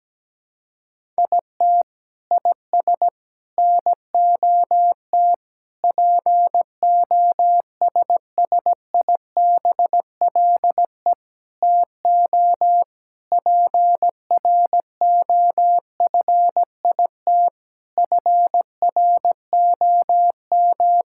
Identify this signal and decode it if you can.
Morse code, heard as a keyed tone, about 17 wpm, 700 Hz. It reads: IT IS NOT POSSIBLE TO PROFIT FROM
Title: Sentences from Top 1000 Words - Review Part 5 - 17wpm